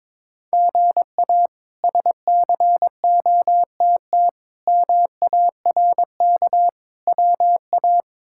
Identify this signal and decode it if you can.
Morse code, heard as a keyed tone, about 22 wpm, 700 Hz. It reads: GA SCOTT MARK WA
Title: K1USN SST QSO Practice - 22wpm